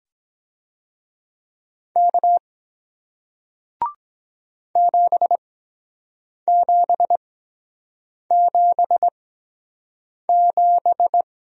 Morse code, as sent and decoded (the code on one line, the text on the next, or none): -.- / --... --... --... --...
K 7777